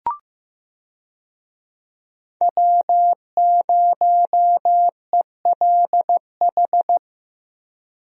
Morse code, heard as a keyed tone, about 15 wpm, 700 Hz. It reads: W0ELH
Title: USA Call Signs - 5 Characters - 15wpm